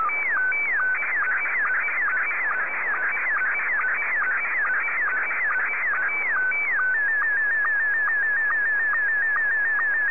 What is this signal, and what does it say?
SSTV (slow-scan television), Scottie 1.